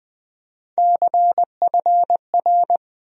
Morse code, heard as a keyed tone, about 20 wpm, 700 Hz. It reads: CFR
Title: USA Call Sign Postfixes - 3 Letters 20wpm